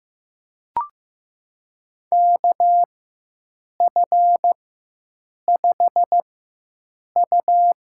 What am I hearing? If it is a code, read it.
Morse code, heard as a keyed tone, about 15 wpm, spaced as for 7 wpm, 700 Hz. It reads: KF5U
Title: USA Call Signs - 6 Characters - 4x Character Spacing - 15wpm